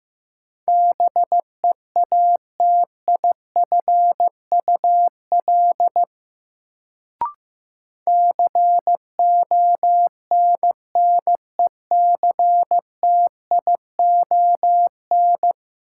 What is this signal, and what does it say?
Morse code, heard as a keyed tone, about 15 wpm, 700 Hz. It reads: BEATIFUL CONNECTION
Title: Sentences from Top 1300 Words - New Word Drill - 15wpm